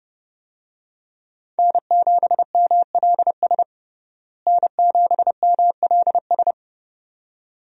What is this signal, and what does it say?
Morse code, heard as a keyed tone, about 30 wpm, 700 Hz. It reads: N7MLS N7MLS
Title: USA Call Signs - 5 Characters - Repeated 2x - 30wpm